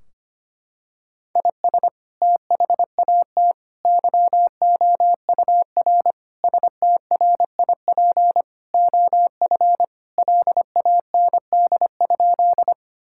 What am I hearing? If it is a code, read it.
Morse code, heard as a keyed tone, about 25 wpm, 700 Hz. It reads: IS THAT YOUR STRIP OF LAND?
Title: Sentences from Top 1600 Words 25wpm